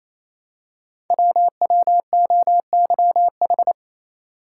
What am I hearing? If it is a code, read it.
Morse code, heard as a keyed tone, about 29 wpm, 700 Hz. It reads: WWOYH
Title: Code Groups - 5 Letters - 28wpm